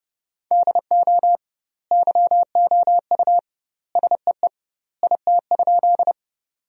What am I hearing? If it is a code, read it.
Morse code, heard as a keyed tone, about 30 wpm, 700 Hz. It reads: DO YOU SEE IT?
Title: 2x Word-Spacing - Sentences from Top 100 Words - 30wpm